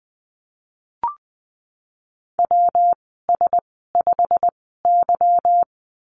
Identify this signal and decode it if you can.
Morse code, heard as a keyed tone, about 20 wpm, spaced as for 15 wpm, 700 Hz. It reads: WS5Y
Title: 161 Lesson 26g - Letter g - Call Signs 20wpm